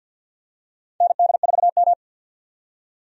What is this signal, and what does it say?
Morse code, heard as a keyed tone, about 53 wpm, 700 Hz. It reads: ND4M
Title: USA Call Signs - 4 Characters - 50wpm